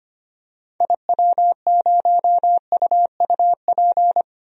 Morse code, heard as a keyed tone, about 25 wpm, 700 Hz. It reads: IW0UUP
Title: DXCC-100 Call Signs - 6 Characters - 25wpm